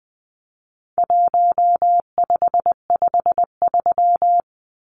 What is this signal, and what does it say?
Morse code, heard as a keyed tone, about 20 wpm, 700 Hz. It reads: 1553
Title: Numbers - 4-Digit 20wpm